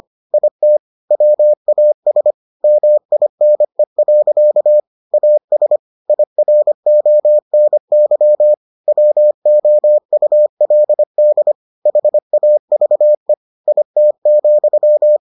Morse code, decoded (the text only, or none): IT WAS MINE. AS IRONY WOULD HAVE IT,